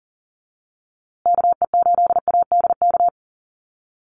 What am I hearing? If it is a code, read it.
Morse code, heard as a keyed tone, about 40 wpm, 700 Hz. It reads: KE8ADK
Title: USA Call Signs Repeated 2x 40wpm